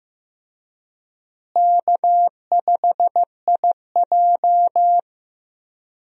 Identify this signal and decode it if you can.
Morse code, heard as a keyed tone, about 15 wpm, 700 Hz. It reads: K5IJ
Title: USA Call Signs - 4 Characters - Repeated 2x - 15wpm